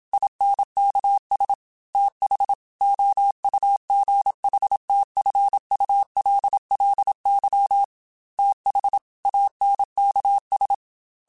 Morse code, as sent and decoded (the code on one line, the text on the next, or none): .. -. -.- ... / - .... / --- ..- --. .... - ..-. ..- .-.. .-.. -.-- / - .... / .- -. -.- ...
INKS TH OUGHTFULLY TH ANKS